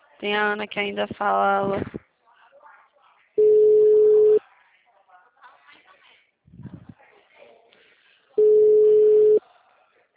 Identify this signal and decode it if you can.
Ringback tone (Europe, 425 Hz), with speech.